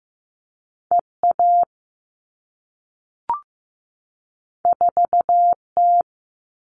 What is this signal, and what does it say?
Morse code, heard as a keyed tone, about 15 wpm, 700 Hz. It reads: EA 4T
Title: All Cut Numbers - 2-Digit 15wpm